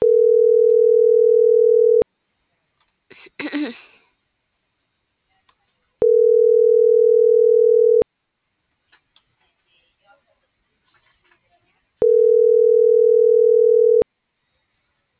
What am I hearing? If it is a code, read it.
Ringback tone (North America), 440 + 480 Hz.